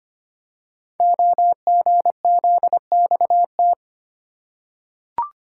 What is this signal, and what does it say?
Morse code, heard as a keyed tone, about 25 wpm, 700 Hz. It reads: OGZXT
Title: Code Groups - 5 Letters - 25wpm